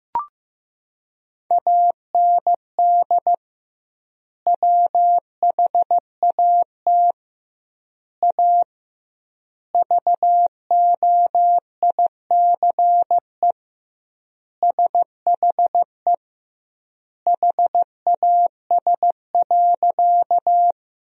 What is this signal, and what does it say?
Morse code, heard as a keyed tone, about 14 wpm, 700 Hz. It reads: AND WHAT A VOICE SHE HAS.
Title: 2x Word-Spacing - Sentences from Top 500 Words - 15wpm